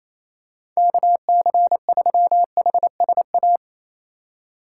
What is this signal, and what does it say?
Morse code, heard as a keyed tone, about 28 wpm, 700 Hz. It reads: KC3HSA
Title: USA Call Signs - 6 Characters - 4x Character Spacing - 28wpm